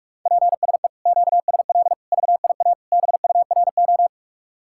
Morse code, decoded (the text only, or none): PSE QSL VIA BURO